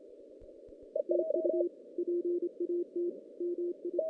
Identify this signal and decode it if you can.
Morse code, heard as a keyed tone, about 29 wpm, 350 Hz. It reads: TU PAT MI
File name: 2016 JAN CW NAQP